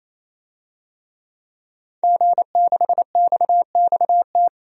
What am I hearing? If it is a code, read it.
Morse code, heard as a keyed tone, about 28 wpm, 700 Hz. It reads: G6XXT
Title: DXCC-100 Call Signs - 5 Characters - 28wpm